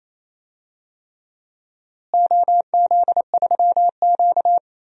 Morse code, heard as a keyed tone, about 27 wpm, 700 Hz. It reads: OZ3Q
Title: DXCC-100 Call Signs - 4 Characters - Repeated 3x - 28wpm